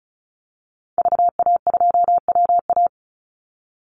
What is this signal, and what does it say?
Morse code, heard as a keyed tone, about 34 wpm, 700 Hz. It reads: VA2WA
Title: Call Signs Repeated 3x - 35wpm